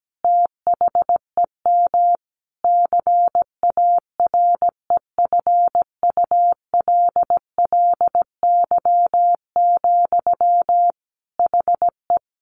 Morse code, heard as a keyed tone, about 17 wpm, 700 Hz. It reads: THEM CAREFULLY, HE
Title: Sentences from Top 1300 Words 17wpm